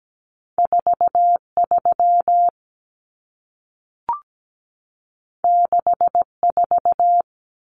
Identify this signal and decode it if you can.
Morse code, heard as a keyed tone, about 17 wpm, 700 Hz. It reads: 43 64
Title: Numbers - 2-Digit 17wpm